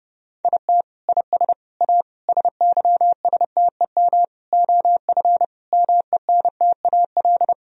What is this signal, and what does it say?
Morse code, heard as a keyed tone, about 30 wpm, 700 Hz. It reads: IT IS A SYSTEM OF MENTAL
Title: Sentences from Top 1000 Words - Review Part 4 - 30wpm